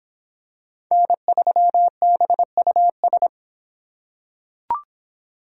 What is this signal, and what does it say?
Morse code, heard as a keyed tone, about 26 wpm, 700 Hz. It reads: N3BUS